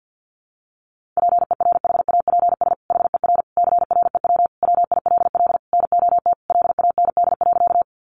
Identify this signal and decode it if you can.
Morse code, heard as a keyed tone, about 50 wpm, 700 Hz. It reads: PERHAPS HER CREW WILL NOT LAND.
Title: Sentences from Top 1500 Words 50wpm